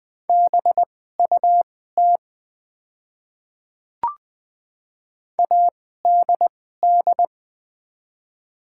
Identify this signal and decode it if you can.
Morse code, heard as a keyed tone, about 20 wpm, spaced as for 15 wpm, 690 Hz. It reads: BUT ADD
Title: 2x Character Spacing - Top 200 Words - 20wpm